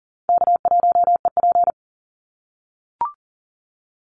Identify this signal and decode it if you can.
Morse code, heard as a keyed tone, about 40 wpm, 700 Hz. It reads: K1EP
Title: Call Signs - 40wpm